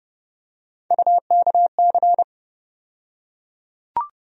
Morse code, heard as a keyed tone, about 30 wpm, 700 Hz. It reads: UKC
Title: USA Call Sign Postfixes - 3 Letters 30wpm